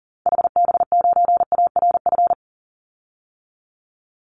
Morse code, heard as a keyed tone, about 40 wpm, 700 Hz. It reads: HB9ARF
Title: Call Signs - 40wpm